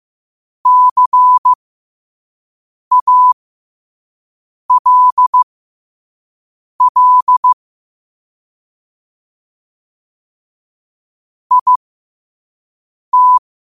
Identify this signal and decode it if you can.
Morse code, heard as a keyed tone, about 15 wpm, spaced as for 5 wpm, 1000 Hz. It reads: CALL IT
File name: qotdfiles / 5wpm